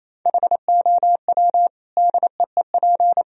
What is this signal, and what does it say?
Morse code, heard as a keyed tone, about 28 wpm, 700 Hz. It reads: HOW DEEP
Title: Sentences from Top 1000 Words - Review Part 3 - 28wpm